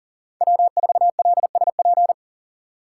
Morse code, heard as a keyed tone, about 40 wpm, 700 Hz. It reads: W4LSP
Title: USA Call Signs - 6x Character Spacing - 40wpm